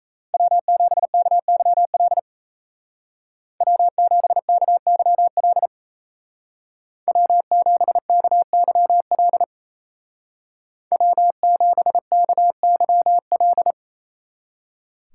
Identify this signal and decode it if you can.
Morse code, heard as a keyed tone, about 33 wpm, 700 Hz. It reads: W7KYL W7KYL W7KYL W7KYL
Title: Speed Racing - USA 1x3 Call Signs 28wpm